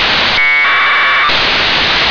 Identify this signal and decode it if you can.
Packet radio (AX.25, as in APRS).
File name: 1k2 ooreos